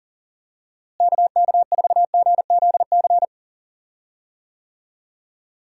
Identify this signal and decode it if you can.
Morse code, heard as a keyed tone, about 39 wpm, 700 Hz. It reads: KK4GZC
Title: USA Call Signs - 6 Characters - Repeated 2x - 40wpm